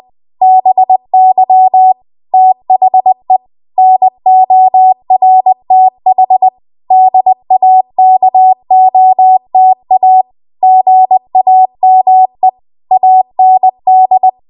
Morse code, heard as a keyed tone, about 20 wpm, 750 Hz. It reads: BY THE NORTH DAKOTA GAME AND
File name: Archive / 20 WPM / 20WPM